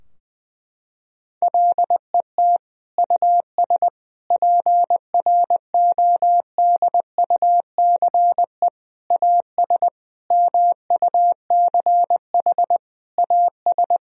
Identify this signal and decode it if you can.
Morse code, heard as a keyed tone, about 21 wpm, 700 Hz. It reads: LET US PRODUCE AS MUCH AS
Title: Sentences from Top 1000 Words - Review Part 3 - 20wpm